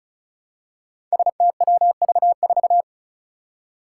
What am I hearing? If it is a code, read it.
Morse code, heard as a keyed tone, about 36 wpm, 700 Hz. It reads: STWV4
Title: Five Challenging Letter-Number Combinations 35wpm